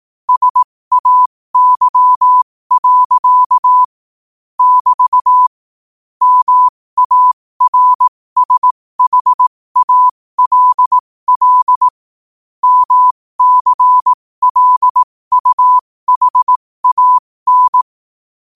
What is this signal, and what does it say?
Morse code, heard as a keyed tone, about 18 wpm, spaced as for 15 wpm, 1000 Hz. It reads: SAY. <BT> MARSHALL MCLUHAN
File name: qotdfiles / 15wpm